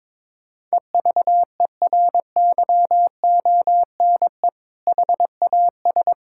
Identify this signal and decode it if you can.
Morse code, heard as a keyed tone, about 21 wpm, 700 Hz. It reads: EVERYONE HAS
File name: practice / 22wpm / Sentences with 2-Word Phrases - 22wpm